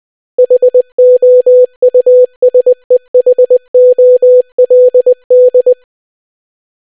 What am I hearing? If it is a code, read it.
Morse code, heard as a keyed tone, about 20 wpm, 500 Hz. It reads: HOUSEHOLD